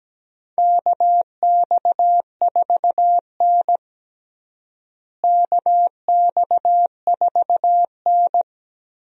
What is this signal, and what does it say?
Morse code, heard as a keyed tone, about 17 wpm, 700 Hz. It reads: KX4N KX4N
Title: USA Call Signs - 4 Characters - Repeated 2x - 17wpm